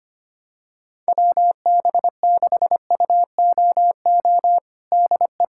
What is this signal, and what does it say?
Morse code, heard as a keyed tone, about 25 wpm, 700 Hz. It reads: WB6UOO DE
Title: USA Callsign DE Callsign 25wpm